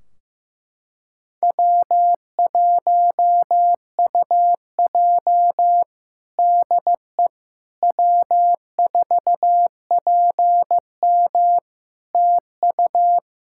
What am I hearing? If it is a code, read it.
Morse code, heard as a keyed tone, about 15 wpm, 700 Hz. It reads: W1UJ DE W4PM TU
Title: K1USN SST QSO Practice 15wpm